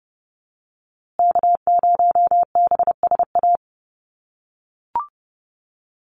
Morse code, heard as a keyed tone, about 30 wpm, 700 Hz. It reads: K0BSA
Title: USA Call Signs Repeated 3x 30wpm